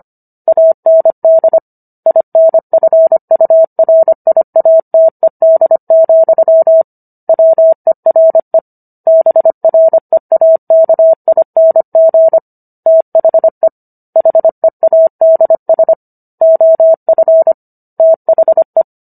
Morse code, heard as a keyed tone, about 25 wpm, 650 Hz. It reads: AND INFURIATED, WERE BREAKING THE HEADS OF THE